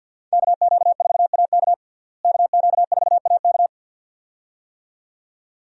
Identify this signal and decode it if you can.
Morse code, heard as a keyed tone, about 52 wpm, 700 Hz. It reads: KQ4AK KQ4AK
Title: USA Call Signs - 5 Characters - Repeated 2x - 50wpm